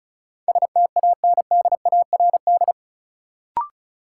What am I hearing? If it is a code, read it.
Morse code, heard as a keyed tone, about 36 wpm, 700 Hz. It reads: STANDARD
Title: Sentences from Top 1200 Words - New Word Drill Alt - 35wpm